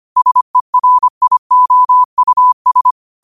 Morse code, heard as a keyed tone, about 25 wpm, 1000 Hz. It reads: SERIOUS